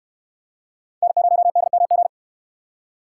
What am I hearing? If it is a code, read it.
Morse code, heard as a keyed tone, about 61 wpm, 700 Hz. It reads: N0DMG